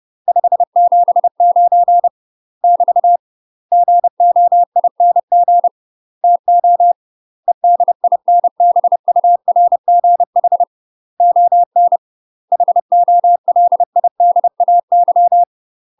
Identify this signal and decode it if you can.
Morse code, heard as a keyed tone, about 30 wpm, 700 Hz. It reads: 579 <BT> GOING TO EDINBURGH ON HOLIDAY
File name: UK MASTER QSOs TRAINING / 30WPM / UK MASTER QSOs 0035